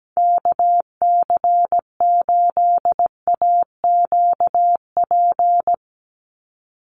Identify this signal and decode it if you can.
Morse code, heard as a keyed tone, about 17 wpm, 700 Hz. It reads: KC8AQP
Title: USA Call Signs 17wpm